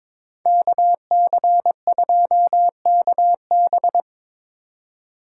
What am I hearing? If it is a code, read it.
Morse code, heard as a keyed tone, about 26 wpm, 700 Hz. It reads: KC2KB
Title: Speed Racing - USA Call Signs 22wpm